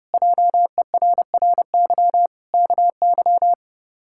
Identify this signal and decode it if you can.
Morse code, heard as a keyed tone, about 30 wpm, 700 Hz. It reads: JERRY KY
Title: K1USN SST QSO Practice - 30wpm